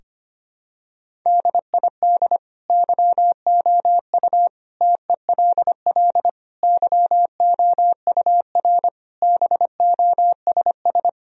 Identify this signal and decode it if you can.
Morse code, heard as a keyed tone, about 25 wpm, 700 Hz. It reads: DID YOU TELL YOUR BOSS
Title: Sentences from Top 1000 Words - Review Part 4 - 25wpm